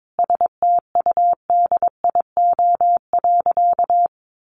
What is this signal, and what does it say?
Morse code, heard as a keyed tone, about 22 wpm, 700 Hz. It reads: STUDIO.
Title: Sentences from Top 1200 Words 22wpm